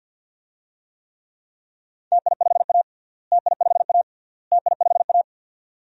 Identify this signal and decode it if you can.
Morse code, heard as a keyed tone, about 49 wpm, 700 Hz. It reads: TI5A TI5A TI5A
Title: DXCC-100 Call Signs - 4 Characters - Repeated 3x - 50wpm